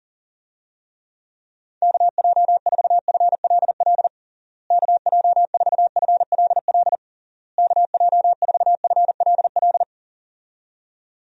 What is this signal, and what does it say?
Morse code, heard as a keyed tone, about 40 wpm, 700 Hz. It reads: KJ4FLL KJ4FLL KJ4FLL
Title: USA Call Signs - 6 Characters - Repeated 3x - 40wpm